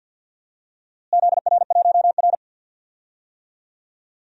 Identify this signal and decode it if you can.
Morse code, heard as a keyed tone, about 52 wpm, 700 Hz. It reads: ZR1R
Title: DXCC-100 Call Signs - 4 Characters - Repeated 3x - 50wpm